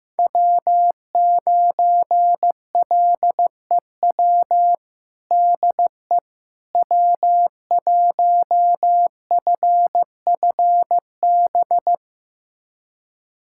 Morse code, heard as a keyed tone, about 15 wpm, 700 Hz. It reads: W9LEW DE W1FFB
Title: USA Callsign DE Callsign - 5 Characters - 15wpm